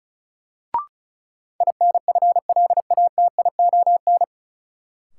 Morse code, heard as a keyed tone, about 34 wpm, 700 Hz. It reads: INFLATION
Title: Sentences from Top 1100 Words - New Word Drill - 35wpm